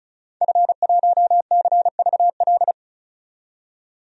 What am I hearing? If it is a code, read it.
Morse code, heard as a keyed tone, about 36 wpm, 700 Hz. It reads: F1CVL
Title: DXCC-100 Call Signs - 5 Characters - 35wpm